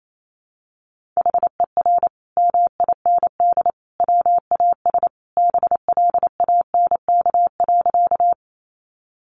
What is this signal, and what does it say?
Morse code, heard as a keyed tone, about 28 wpm, 700 Hz. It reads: HER MIND WAS BLANK.
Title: Sentences from Top 1500 Words 28wpm